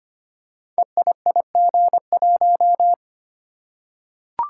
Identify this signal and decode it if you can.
Morse code, heard as a keyed tone, about 25 wpm, 700 Hz. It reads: EIIG1
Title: Code Groups - 5 Letter:Numbers - 25wpm